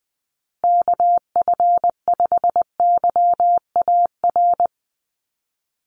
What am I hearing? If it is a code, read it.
Morse code, heard as a keyed tone, about 20 wpm, 700 Hz. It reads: KF5YAR
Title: USA Call Signs Repeated 3x 20wpm